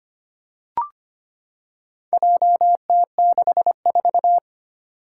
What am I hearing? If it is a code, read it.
Morse code, heard as a keyed tone, about 25 wpm, 700 Hz. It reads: JT64
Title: Four Challenging Letter-Number Combinations 25wpm